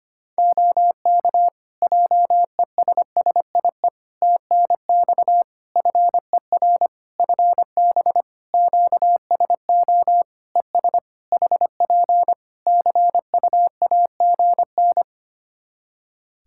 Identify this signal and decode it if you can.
Morse code, heard as a keyed tone, about 25 wpm, 700 Hz. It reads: OK JESSIE TNX FER FB QSO ES HP CUAGN
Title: Practice QSO - Exchange 4 - Ending - 25wpm